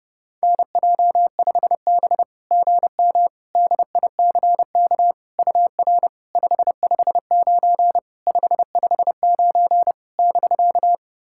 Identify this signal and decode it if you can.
Morse code, heard as a keyed tone, about 30 wpm, 700 Hz. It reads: NJ5B GM DICK UR 559 559 <BK>
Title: SOTA QSO Practice 30wpm